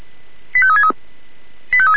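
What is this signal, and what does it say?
Five-tone selective call (ZVEI), heard as quick runs of five tones, tones of about 70 ms. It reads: 85363
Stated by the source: ZVEI five-tone call 85363, received by SDR.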